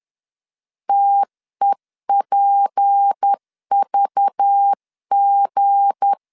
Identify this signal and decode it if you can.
Morse code, heard as a keyed tone, about 10 wpm, 800 Hz. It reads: TEPVG
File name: LEZIONI / 7 GF9Z / 7 G 10wpm 2501